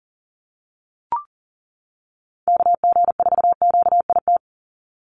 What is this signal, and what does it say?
Morse code, heard as a keyed tone, about 40 wpm, 690 Hz. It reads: KG4QIT
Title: USA Call Signs 40wpm